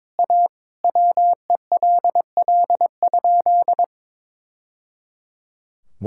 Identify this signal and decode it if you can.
Morse code, heard as a keyed tone, about 22 wpm, 700 Hz. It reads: A WELL?
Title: Sentences from Top 1000 Words - Review Part 6 - 22wpm